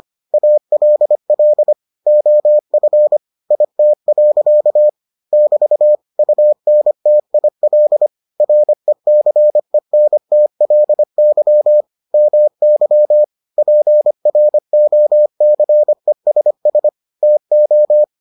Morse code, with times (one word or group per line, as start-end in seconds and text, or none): 0.3-1.7 ALL
2.1-3.2 OF
3.5-4.9 IT.
5.3-6.0 <BT>
6.2-8.1 UNTIL
8.4-11.8 RECENTLY
12.1-13.2 MY
13.6-16.9 PROCESS
17.2-18.0 TO